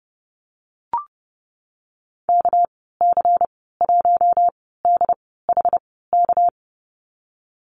Morse code, heard as a keyed tone, about 31 wpm, spaced as for 17 wpm, 700 Hz. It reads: KC1DHK
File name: Lesson 37g - Letter x - Call Signs 30wpm